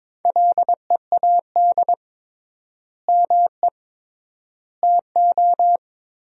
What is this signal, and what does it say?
Morse code, heard as a keyed tone, about 22 wpm, 700 Hz. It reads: LEAD ME TO
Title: 3x Word-Spacing - Sentences from Top 500 Words - 22wpm